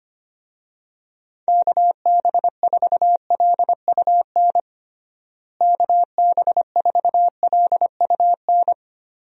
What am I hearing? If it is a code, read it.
Morse code, heard as a keyed tone, about 25 wpm, 700 Hz. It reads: KB4LUN KB4LUN
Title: USA Call Signs - 6 Characters - Repeated 2x - 25wpm